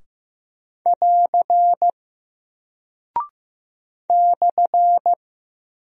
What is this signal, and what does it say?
Morse code, heard as a keyed tone, about 15 wpm, 700 Hz. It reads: <AR> /
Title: Single Character (Letter:Number:Punctuation:Prosign) Alt 15wpm